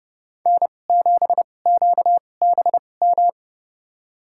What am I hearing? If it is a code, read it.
Morse code, heard as a keyed tone, about 32 wpm, spaced as for 22 wpm, 700 Hz. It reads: N7QBM
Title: USA Call Signs - 5 Characters - 2x Character Spacing - 30wpm